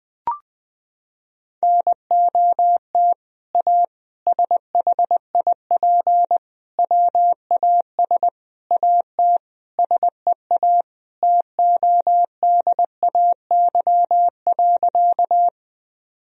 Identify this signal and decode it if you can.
Morse code, heard as a keyed tone, about 21 wpm, 700 Hz. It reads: NOT A SHIP WAS AT SEA TODAY.
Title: Sentences from Top 1000 Words - Review Part 2 - 20wpm